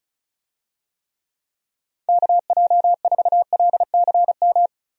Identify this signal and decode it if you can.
Morse code, heard as a keyed tone, about 34 wpm, 700 Hz. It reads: KJ4LCM
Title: USA Call Signs - 6 Characters - 35wpm